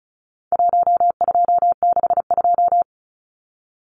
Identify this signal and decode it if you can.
Morse code, heard as a keyed tone, about 34 wpm, 700 Hz. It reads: 1262
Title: Numbers - 4-Digit 35wpm